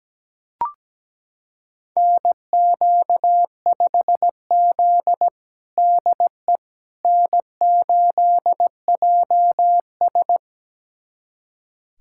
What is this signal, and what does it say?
Morse code, heard as a keyed tone, about 17 wpm, 700 Hz. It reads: NQ5Z DE N8JS
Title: USA Callsign DE Callsign - 4 Characters - 17wpm